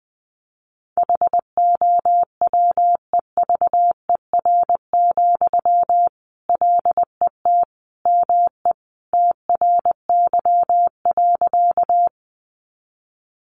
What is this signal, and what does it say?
Morse code, heard as a keyed tone, about 20 wpm, 700 Hz. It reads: HOWEVER, LET ME TRY.
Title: Sentences from Top 600 Words 20wpm